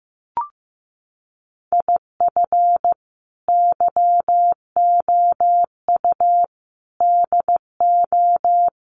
Morse code, heard as a keyed tone, about 15 wpm, 700 Hz. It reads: IF YOU DO
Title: Sets of 5 Words 15wpm